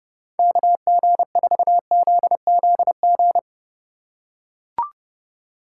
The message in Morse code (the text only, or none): KG4ZZG